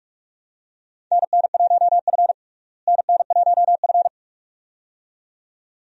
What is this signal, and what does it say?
Morse code, heard as a keyed tone, about 42 wpm, 700 Hz. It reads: NN1F NN1F
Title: USA Call Signs - 4 Characters - Repeated 2x - 45wpm